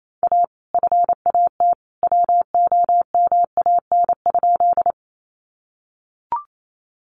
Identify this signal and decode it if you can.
Morse code, heard as a keyed tone, about 28 wpm, 700 Hz. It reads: A FAT WOMAN?
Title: Sentences from Top 1100 Words 28wpm